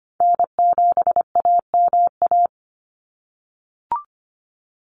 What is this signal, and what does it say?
Morse code, heard as a keyed tone, about 25 wpm, 700 Hz. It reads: N7AMA
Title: NA SOTA Chaser Callsigns 25wpm